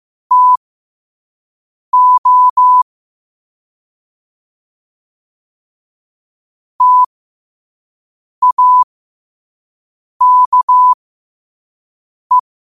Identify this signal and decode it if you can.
Morse code, heard as a keyed tone, about 15 wpm, spaced as for 5 wpm, 1000 Hz. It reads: TO TAKE
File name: qotdfiles / 5wpm